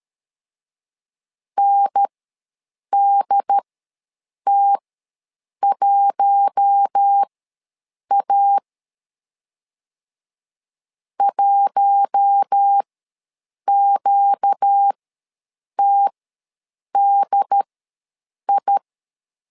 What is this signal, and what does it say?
Morse code, heard as a keyed tone, about 13 wpm, spaced as for 7 wpm, 800 Hz. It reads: NDT1A 1QTDI
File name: LEZIONI / 4 Q1 ws 12wpm 2129